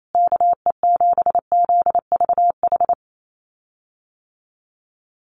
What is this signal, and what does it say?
Morse code, heard as a keyed tone, about 28 wpm, 700 Hz. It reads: KE7ZVH
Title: USA Call Signs Repeated 3x 28wpm